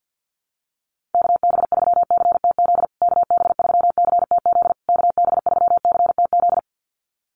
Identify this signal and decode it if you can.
Morse code, heard as a keyed tone, about 50 wpm, 700 Hz. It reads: KB3CTZ KB3CTZ KB3CTZ
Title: USA Call Signs Repeated 3x 50wpm